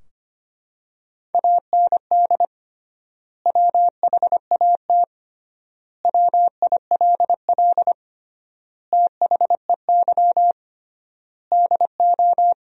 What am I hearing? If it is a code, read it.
Morse code, heard as a keyed tone, about 25 wpm, 700 Hz. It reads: AND WHAT WILL THEY DO
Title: 3x Word-Spacing - Sentences from Top 200 Words - 25wpm